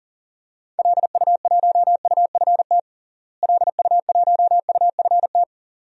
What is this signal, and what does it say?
Morse code, heard as a keyed tone, about 37 wpm, 700 Hz. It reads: LU1UFT LU1UFT
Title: DXCC-100 Call Signs - 6 Characters - Repeated 2x - 40wpm